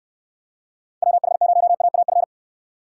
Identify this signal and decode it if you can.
Morse code, heard as a keyed tone, about 69 wpm, 700 Hz. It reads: WH0AAF